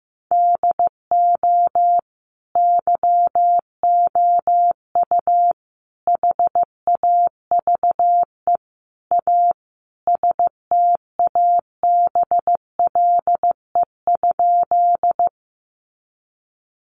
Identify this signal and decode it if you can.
Morse code, heard as a keyed tone, about 15 wpm, 700 Hz. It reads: DO YOU HAVE A STABLE?
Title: Sentences from Top 1500 Words 15wpm